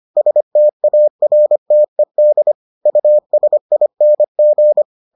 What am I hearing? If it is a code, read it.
Morse code, heard as a keyed tone, about 25 wpm, 600 Hz. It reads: STARTED USING